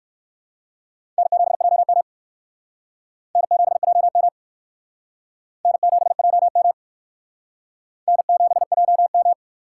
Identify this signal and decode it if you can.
Morse code, heard as a keyed tone, about 62 wpm, 700 Hz. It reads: N7JM N7JM N7JM N7JM